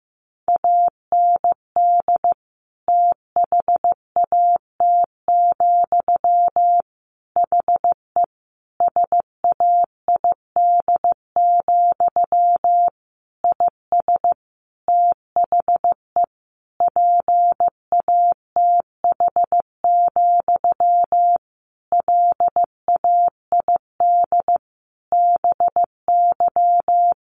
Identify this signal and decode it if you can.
Morse code, heard as a keyed tone, about 15 wpm, 700 Hz. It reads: AND THAT, HE SAID, IS THE PATH, LAID BY
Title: A Sound Of Thunder 15wpm